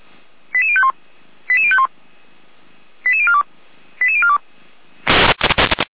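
Five-tone selective call (ZVEI), heard as quick runs of five tones, tones of about 70 ms. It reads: 88051 88051 88042 88042
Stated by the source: ZVEI five-tone selective call received via SDR.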